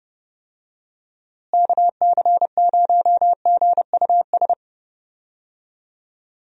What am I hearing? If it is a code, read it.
Morse code, heard as a keyed tone, about 28 wpm, 700 Hz. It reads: KC0GUS
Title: USA Call Signs - 6 Characters - Repeated 2x - 30wpm